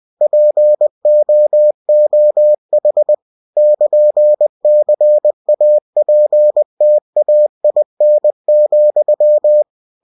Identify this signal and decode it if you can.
Morse code, heard as a keyed tone, about 20 wpm, 600 Hz. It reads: POOH <KN> CAPTAIN,
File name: cw-books / winnie-the-pooh / winnie-20wpm-page-0101